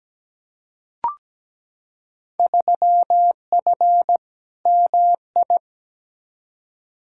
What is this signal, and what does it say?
Morse code, heard as a keyed tone, about 17 wpm, 700 Hz. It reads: 3F MI
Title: Field Day Exchanges - 17wpm